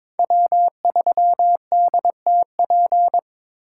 Morse code, heard as a keyed tone, about 22 wpm, 700 Hz. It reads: W3DTP
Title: USA Call Signs - 5 Characters - 2x Character Spacing - 22wpm